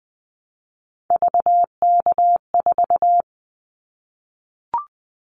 Morse code, heard as a keyed tone, about 20 wpm, 700 Hz. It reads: VK4
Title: Callsign Prefixes for Top 100 DXCC - 20wpm